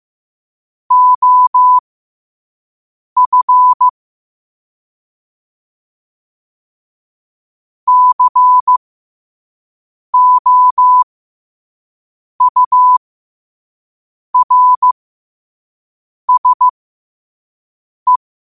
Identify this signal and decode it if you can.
Morse code, heard as a keyed tone, about 15 wpm, spaced as for 5 wpm, 1000 Hz. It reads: OF COURSE
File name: qotdfiles / 5wpm